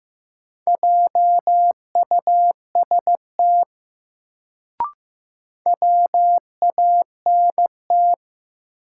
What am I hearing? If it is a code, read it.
Morse code, heard as a keyed tone, about 15 wpm, 700 Hz. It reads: JUST WANT
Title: Sentences from Top 200 Words - New Word Drill - 15wpm